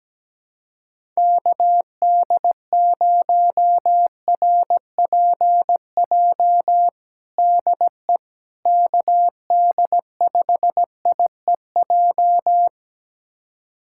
Morse code, heard as a keyed tone, about 17 wpm, 700 Hz. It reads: KD0RPJ DE KD5IEJ
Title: USA Callsign DE Callsign - 6 Characters - 17wpm